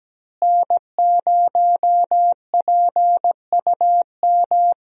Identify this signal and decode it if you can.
Morse code, heard as a keyed tone, about 17 wpm, 700 Hz. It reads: N0PUM
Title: USA Call Signs - 5 Characters - Repeated 2x - 17wpm